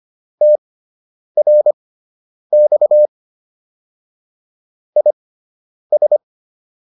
Morse code, heard as a keyed tone, about 25 wpm, spaced as for 9 wpm, 600 Hz. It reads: TRX IS